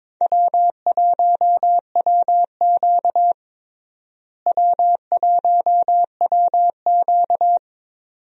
Morse code, heard as a keyed tone, about 22 wpm, 700 Hz. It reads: W1WQ W1WQ
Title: USA Call Signs - 4 Characters - Repeated 2x - 22wpm